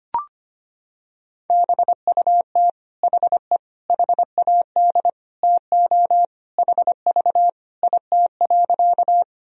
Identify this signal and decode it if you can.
Morse code, heard as a keyed tone, about 25 wpm, 700 Hz. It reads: BUT HE HAD TO HV IT.
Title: Sentences from Top 100 Words and Abbreviations 25wpm